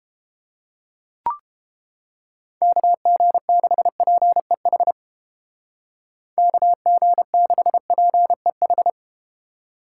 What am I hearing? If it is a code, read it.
Morse code, heard as a keyed tone, about 32 wpm, 700 Hz. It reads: KG6PEH KG6PEH